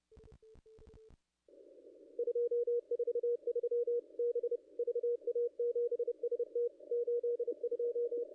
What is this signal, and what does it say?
Morse code, heard as a keyed tone, about 29 wpm, 480 Hz. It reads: 243 B VA7ST 82
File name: contests / ss-cw-k5tr-op-n5rz -0100